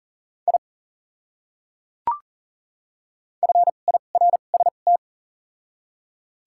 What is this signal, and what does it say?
Morse code, heard as a keyed tone, about 40 wpm, spaced as for 29 wpm, 700 Hz. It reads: I FIRST
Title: 2x Character Spacing - Top 100 Words - 40wpm